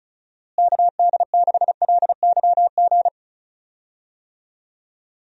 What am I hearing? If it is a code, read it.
Morse code, heard as a keyed tone, about 36 wpm, 700 Hz. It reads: KD6LYG
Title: USA Call Signs - 6 Characters - Repeated 2x - 35wpm